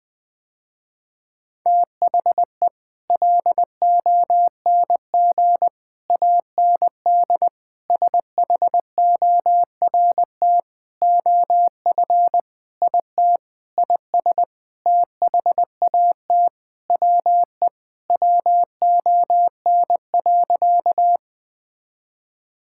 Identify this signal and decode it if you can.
Morse code, heard as a keyed tone, about 19 wpm, 700 Hz. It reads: THE LONG AND SHORT OF IT IS THAT WE WON.
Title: Sentences with 8-Word Phrases - Spoken Beginning - 20wpm